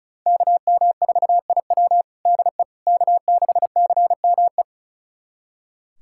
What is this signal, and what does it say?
Morse code, heard as a keyed tone, about 34 wpm, 690 Hz. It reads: KM4IW DE K6CME
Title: USA Callsign DE Callsign - 5 Characters - 35wpm